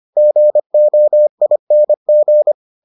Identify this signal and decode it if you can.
Morse code, heard as a keyed tone, about 25 wpm, 600 Hz. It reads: GOING